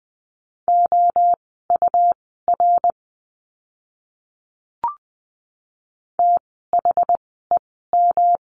Morse code, heard as a keyed tone, about 20 wpm, spaced as for 14 wpm, 700 Hz. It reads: OUR THEM
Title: 160 Lesson 26f - Letter g - Words 20wpm